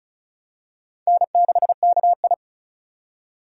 Morse code, heard as a keyed tone, about 34 wpm, 700 Hz. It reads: N6KI
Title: USA Call Signs - 3x Character Spacing - 35wpm